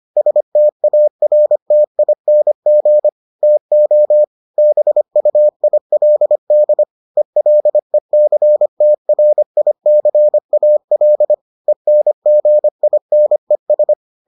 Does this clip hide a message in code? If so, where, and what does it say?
Yes: Morse code from 0.2 to 13.9 s, heard as a keyed tone, about 25 wpm, 600 Hz. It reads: STARTING TO BUILD ELECTRICAL ENGINES